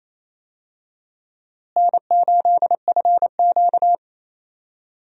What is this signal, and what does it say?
Morse code, heard as a keyed tone, about 28 wpm, 700 Hz. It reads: N8FQ
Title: USA Call Signs - 4 Characters - Repeated 3x - 28wpm